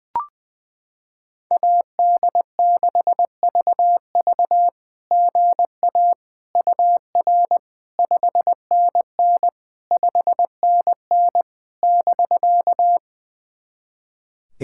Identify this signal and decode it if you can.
Morse code, heard as a keyed tone, about 21 wpm, 700 Hz. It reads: AD6VV GA UR 5NN 5NN <BK>
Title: SOTA QSO Practice 20wpm